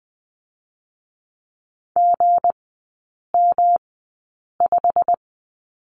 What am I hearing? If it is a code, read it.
Morse code, heard as a keyed tone, about 20 wpm, spaced as for 8 wpm, 690 Hz. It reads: GM5
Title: 212 Lesson 34e - Number 8 - Three Characters 20wpm